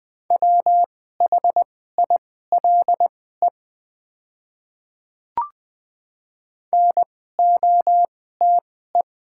Morse code, heard as a keyed tone, about 20 wpm, spaced as for 14 wpm, 690 Hz. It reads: WHILE NOTE
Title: 2x Character Spacing - Top 500 Words - 20wpm